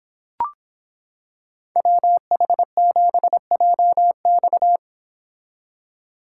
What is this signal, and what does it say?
Morse code, heard as a keyed tone, about 25 wpm, 700 Hz. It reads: WH7JX